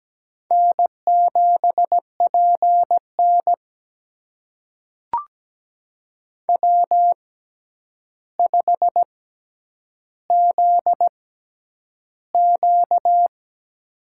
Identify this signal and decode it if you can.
Morse code, heard as a keyed tone, about 17 wpm, spaced as for 9 wpm, 700 Hz. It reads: N7PN W5ZQ
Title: USA Call Signs - 6x Character Spacing - 17wpm